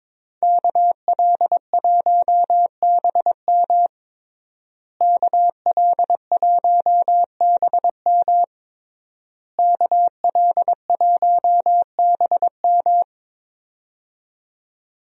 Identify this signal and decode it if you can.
Morse code, heard as a keyed tone, about 21 wpm, 700 Hz. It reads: KL1BM KL1BM KL1BM
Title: USA Call Signs - 5 Characters - Repeated 3x - 22wpm